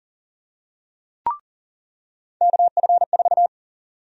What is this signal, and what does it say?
Morse code, heard as a keyed tone, about 40 wpm, 700 Hz. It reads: KF4
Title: USA Call Sign Prefixes - 40wpm